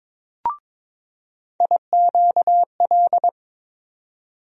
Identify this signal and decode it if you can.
Morse code, heard as a keyed tone, about 22 wpm, 700 Hz. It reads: IQL
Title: USA Call Sign Postfixes - 3 Letters 22wpm